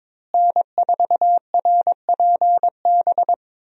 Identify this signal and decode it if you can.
Morse code, heard as a keyed tone, about 22 wpm, 700 Hz. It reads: N4RPB
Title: USA Call Signs - 5 Characters - Repeated 3x - 22wpm